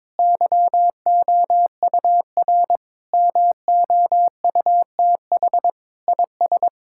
Morse code, heard as a keyed tone, about 22 wpm, 700 Hz. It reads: YOUR MOUTH IS
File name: practice / 22wpm / Sets of 7 Words - 22wpm